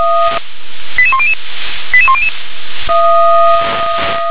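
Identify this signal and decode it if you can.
Five-tone selective call (ZVEI), heard as quick runs of five tones, tones of about 70 ms. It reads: 88100 88100
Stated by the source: ZVEI five-tone selective call 88835 received on an SDR.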